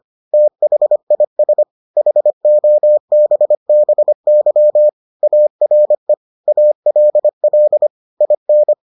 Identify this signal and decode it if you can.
Morse code, heard as a keyed tone, about 25 wpm, 600 Hz. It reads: THIS HOBBY ARE ALL IN